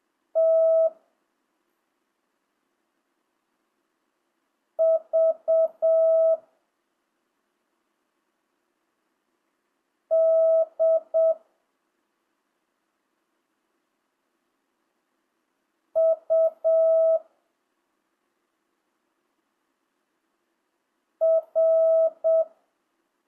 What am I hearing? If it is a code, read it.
Morse code, heard as a keyed tone, about 7 wpm, 660 Hz. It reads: T V D U R
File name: jh8cbh / 2025cw